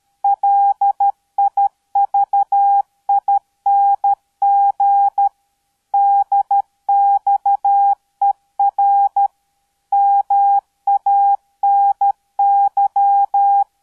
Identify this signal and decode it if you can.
Morse code, heard as a keyed tone, about 13 wpm, 800 Hz. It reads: LIVING DXER MANY